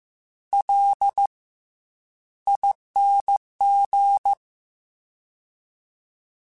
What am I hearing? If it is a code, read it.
Morse code, heard as a keyed tone, about 15 wpm, 800 Hz. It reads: L ING
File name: Special INGs-mp3s / Special INGs-mp3s / Fast ing @13